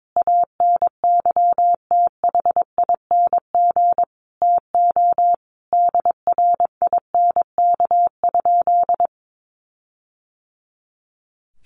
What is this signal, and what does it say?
Morse code, heard as a keyed tone, about 22 wpm, 700 Hz. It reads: ANYTHING TO DRINK?
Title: Sentences from Top 1500 Words 22wpm